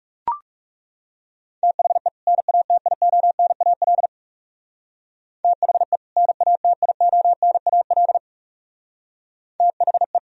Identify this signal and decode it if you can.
Morse code, heard as a keyed tone, about 36 wpm, 700 Hz. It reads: THE NATIONAL THE NATIONAL THE
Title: Speed-Racing - Sets of 2 Words 30wpm